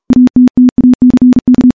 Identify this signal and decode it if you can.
Morse code, heard as a keyed tone, about 40 wpm, 250 Hz. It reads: ATTACK
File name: morse / morse-training-40WPM